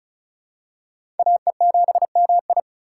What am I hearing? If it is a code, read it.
Morse code, heard as a keyed tone, about 34 wpm, 700 Hz. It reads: AE7MI
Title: USA Call Signs - 5 Characters - 3x Character Spacing - 35wpm